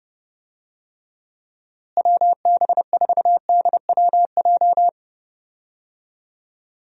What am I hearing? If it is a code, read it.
Morse code, heard as a keyed tone, about 30 wpm, 700 Hz. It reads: WB4DWJ
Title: USA Call Signs - 6 Characters - Repeated 2x - 30wpm